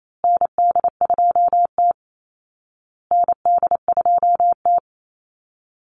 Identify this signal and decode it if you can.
Morse code, heard as a keyed tone, about 28 wpm, 700 Hz. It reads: ND2T ND2T
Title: Call Signs Repeated 2x - 28wpm